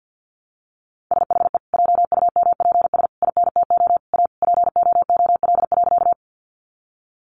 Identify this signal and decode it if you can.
Morse code, heard as a keyed tone, about 50 wpm, 700 Hz. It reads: SHE JUMPS INTO A POOL.
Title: Easy Phrases 50wpm